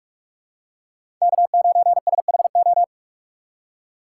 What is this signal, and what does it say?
Morse code, heard as a keyed tone, about 42 wpm, 700 Hz. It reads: K9SHO